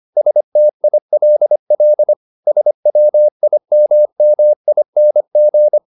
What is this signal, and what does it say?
Morse code, heard as a keyed tone, about 25 wpm, 600 Hz. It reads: STILL SWIMMING
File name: the old man and the sea part 1 25wpm